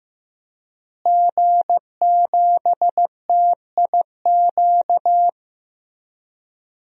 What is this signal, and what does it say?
Morse code, heard as a keyed tone, about 15 wpm, 700 Hz. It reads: G7TIQ
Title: DXCC-100 Call Signs - 5 Characters - Repeated 3x - 15wpm